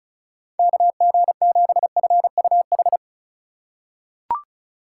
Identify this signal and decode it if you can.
Morse code, heard as a keyed tone, about 34 wpm, 700 Hz. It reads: KG7FUH
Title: USA Call Signs - 2x Character Spacing - 35wpm